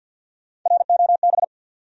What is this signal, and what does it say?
Morse code, heard as a keyed tone, about 53 wpm, 700 Hz. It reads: ROB
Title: USA Common Names - 50wpm